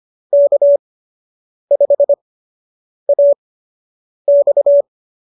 Morse code, heard as a keyed tone, about 25 wpm, spaced as for 8 wpm, 570 Hz. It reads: K5AX